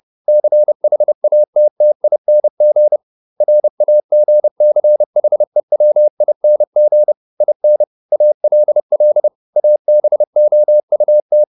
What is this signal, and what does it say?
Morse code, heard as a keyed tone, about 30 wpm, 600 Hz. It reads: CHATTING RAGCHEWING IN ALL ABOUT